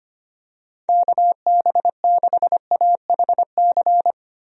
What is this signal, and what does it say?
Morse code, heard as a keyed tone, about 25 wpm, 700 Hz. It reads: KB6AHC
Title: USA Call Signs - 6 Characters - 25wpm